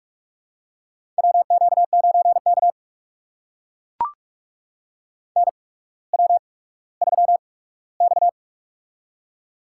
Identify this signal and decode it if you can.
Morse code, heard as a keyed tone, about 44 wpm, spaced as for 19 wpm, 700 Hz. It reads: WQ9K NW3X
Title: USA Call Signs - 4 Characters - 8x Character Spacing - 45wpm